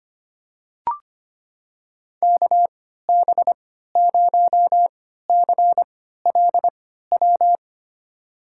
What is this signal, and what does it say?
Morse code, heard as a keyed tone, about 25 wpm, spaced as for 14 wpm, 700 Hz. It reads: KB0CLW
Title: USA Call Signs - 6 Characters - 3x Character Spacing - 25wpm